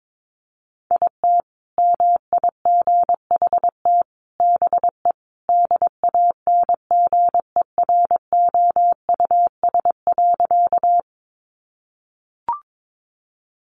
Morse code, heard as a keyed tone, about 22 wpm, 700 Hz. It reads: IT MIGHT BE DANGEROUS.
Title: Sentences from Top 1200 Words 22wpm